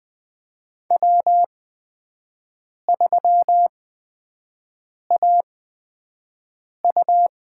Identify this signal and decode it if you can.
Morse code, heard as a keyed tone, about 20 wpm, spaced as for 5 wpm, 700 Hz. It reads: W3AU
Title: USA Call Signs - 4 Characters - 8x Character Spacing - 20wpm